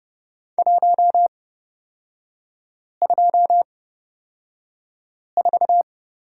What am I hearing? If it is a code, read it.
Morse code, heard as a keyed tone, about 30 wpm, spaced as for 5 wpm, 700 Hz. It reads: 124
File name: Lesson 36e - BT - Three Characters 30wpm